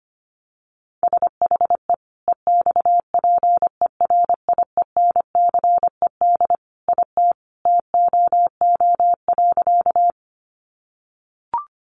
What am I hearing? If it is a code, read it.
Morse code, heard as a keyed tone, about 25 wpm, 700 Hz. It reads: SHE EXPERIENCED IT TOO.
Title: Sentences from Top 1100 Words 25wpm